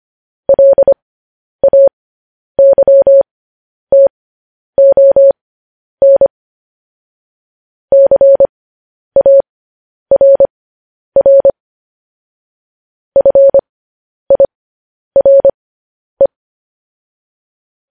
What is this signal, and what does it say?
Morse code, heard as a keyed tone, about 25 wpm, spaced as for 10 wpm, 550 Hz. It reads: LAYTON CARR FIRE
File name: podcast / News headlines-1x-25x10